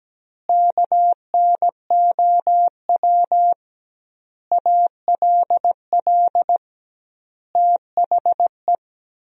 Morse code, heard as a keyed tone, about 17 wpm, 700 Hz. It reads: KNOW ALL THE
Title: 2x Word-Spacing - Sentences from Top 300 Words - 17wpm